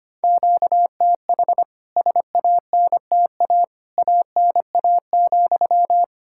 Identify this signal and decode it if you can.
Morse code, heard as a keyed tone, about 25 wpm, 700 Hz. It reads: QTH SANTA ANA,
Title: K1USN SST QSO Practice 25wpm